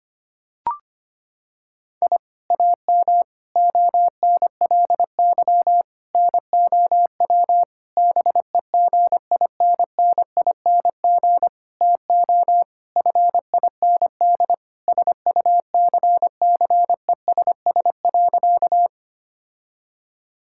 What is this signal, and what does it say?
Morse code, heard as a keyed tone, about 25 wpm, 700 Hz. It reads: I AM ONLY NOW BEGINNING TO FIND SUCCESS.
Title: Sentences from Top 1000 Words - Review Part 5 - 25wpm